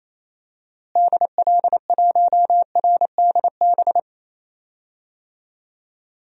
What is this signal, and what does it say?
Morse code, heard as a keyed tone, about 28 wpm, 700 Hz. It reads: DL1RDB
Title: DXCC-100 Call Signs - 6 Characters - Repeated 3x - 28wpm